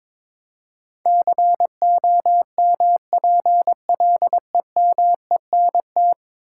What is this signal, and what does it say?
Morse code, heard as a keyed tone, about 22 wpm, 700 Hz. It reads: COMPLEMENT
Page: morse-code-ninja-practice-22wpm